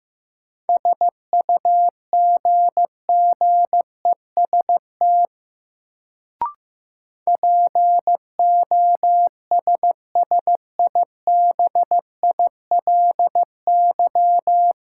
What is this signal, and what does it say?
Morse code, heard as a keyed tone, about 15 wpm, 700 Hz. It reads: SUGGEST POSSIBILY
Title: Sentences from Top 1300 Words - New Word Drill Alt - 15wpm